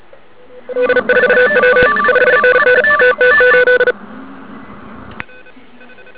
Morse code, heard as a keyed tone, about 37 wpm, 500 Hz, with noise. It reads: D4W 5NN TT8
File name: ARCHIV / 80m cw